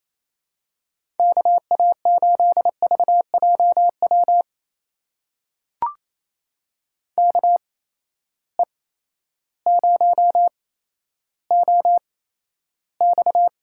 Morse code, heard as a keyed tone, about 27 wpm, spaced as for 13 wpm, 700 Hz. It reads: KA8VJW KE0OX
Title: USA Call Signs - 6 Characters - 8x Character Spacing - 28wpm